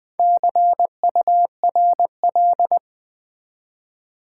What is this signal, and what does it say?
Morse code, heard as a keyed tone, about 20 wpm, 700 Hz. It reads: CURL
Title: Build A Word - 1 Letter At A Time - 5-Letter Words - 20wpm